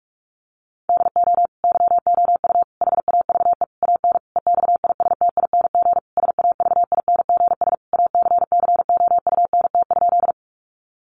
Morse code, heard as a keyed tone, about 44 wpm, 700 Hz. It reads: DO YOU HAVE AN EXISTING SAVINGS ACCOUNT?
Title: Sentences from Top 1500 Words 45wpm